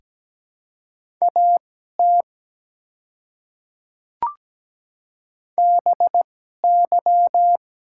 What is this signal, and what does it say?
Morse code, heard as a keyed tone, about 17 wpm, spaced as for 12 wpm, 700 Hz. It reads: AT BY
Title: 2x Character Spacing - Top 100 Words - 17wpm